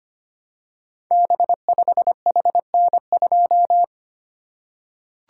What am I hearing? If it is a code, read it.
Morse code, heard as a keyed tone, about 25 wpm, 700 Hz. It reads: B5HN2
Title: Code Groups - 5 Letter:Numbers - 25wpm